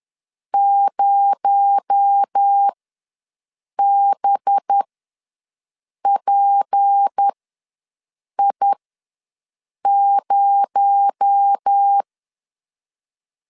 Morse code, heard as a keyed tone, about 10 wpm, spaced as for 6 wpm, 800 Hz. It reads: 0BPI0
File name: LEZIONI / 6 PKX0 / 6 PKX0 ws 10wpm 4220